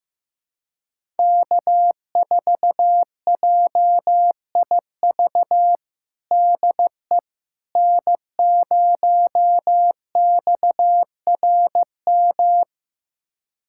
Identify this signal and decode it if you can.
Morse code, heard as a keyed tone, about 15 wpm, 700 Hz. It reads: K4JIV DE N0XRM
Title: USA Callsign DE Callsign - 5 Characters - 15wpm